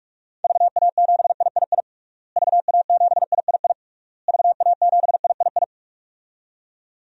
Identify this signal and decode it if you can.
Morse code, heard as a keyed tone, about 44 wpm, 700 Hz. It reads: VA7III VA7III VA7III
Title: DXCC-100 Call Signs - 6 Characters - Repeated 3x - 45wpm